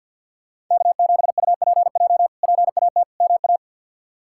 Morse code, heard as a keyed tone, about 47 wpm, 690 Hz. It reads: K7UPJ PAT MA
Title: K1USN SST QSO Practice - 50wpm